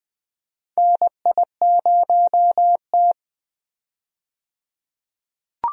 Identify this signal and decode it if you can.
Morse code, heard as a keyed tone, about 21 wpm, 700 Hz. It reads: NI0T
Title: USA Call Signs - 4 Characters - 20wpm